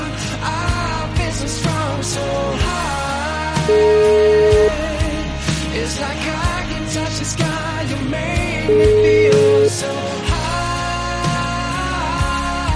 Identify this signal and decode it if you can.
Ringback tone (Europe, 425 Hz), with noise.